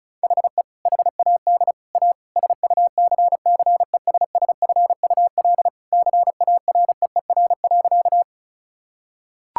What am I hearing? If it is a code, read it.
Morse code, heard as a keyed tone, about 36 wpm, 690 Hz. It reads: HE HAD A SUCCESSFUL CAREER.
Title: Sentences from Top 1000 Words - Review Part 6 - 35wpm